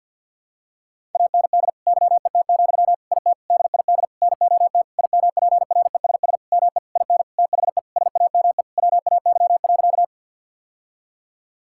Morse code, heard as a keyed tone, about 50 wpm, 700 Hz. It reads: AND YET, IT DID NOT IMPRESS ME IN THE SAME WAY.
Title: Sentences from Top 1600 Words 50wpm